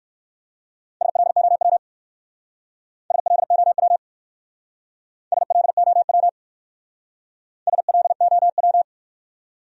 Morse code, heard as a keyed tone, about 60 wpm, 700 Hz. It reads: SLOW SLOW SLOW SLOW